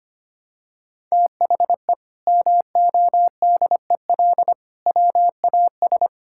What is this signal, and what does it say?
Morse code, heard as a keyed tone, about 25 wpm, 700 Hz. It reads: THE MODEL WAS
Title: Sentences from Top 1000 Words - Review Part 5 - 25wpm